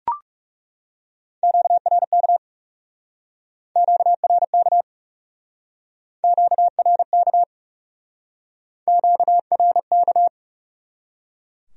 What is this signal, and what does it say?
Morse code, heard as a keyed tone, about 38 wpm, 700 Hz. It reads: QRK QRK QRK QRK